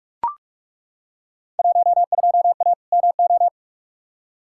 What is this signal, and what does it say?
Morse code, heard as a keyed tone, about 43 wpm, 700 Hz. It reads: 12A MO
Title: Field Day Exchanges Encore - 45wpm